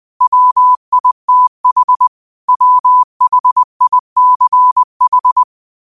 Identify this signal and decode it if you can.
Morse code, heard as a keyed tone, about 20 wpm, 1000 Hz. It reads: WITH WHICH
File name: qotdfiles / 20wpm